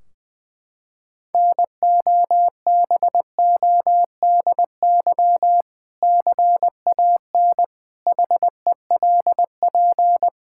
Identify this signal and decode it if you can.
Morse code, heard as a keyed tone, about 20 wpm, 700 Hz. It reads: NOBODY CAN HELP
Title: Sentences from Top 1600 Words 20wpm